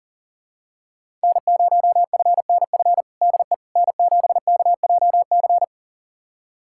Morse code, heard as a keyed tone, about 40 wpm, 690 Hz. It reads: N0FNF DE N7KJC
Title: USA Callsign DE Callsign - 5 Characters - 40wpm